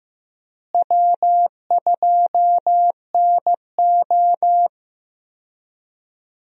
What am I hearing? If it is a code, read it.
Morse code, heard as a keyed tone, about 15 wpm, 700 Hz. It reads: W2NO
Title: USA Call Signs - 4 Characters - Repeated 2x - 15wpm